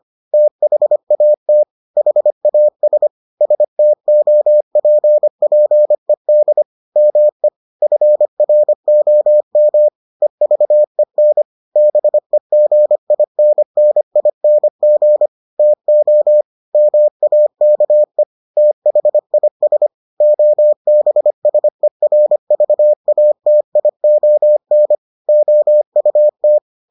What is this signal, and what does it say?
Morse code, heard as a keyed tone, about 25 wpm, 600 Hz. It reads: THAT HAS STOPPED ME FROM EVEN BEGINNING TO MAKE THIS OBSERVATION OUT